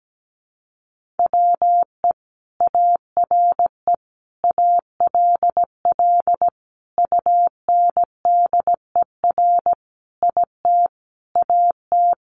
Morse code, heard as a keyed tone, about 17 wpm, 700 Hz. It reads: WE ARE ALL UNDER IT AT
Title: Sentences from Top 1500 Words 17wpm